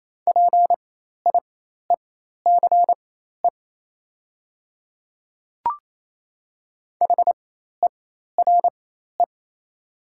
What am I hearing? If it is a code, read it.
Morse code, heard as a keyed tone, about 28 wpm, spaced as for 13 wpm, 690 Hz. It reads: PIECE HERE
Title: 4x Character Spacing - Top 500 Words - 28wpm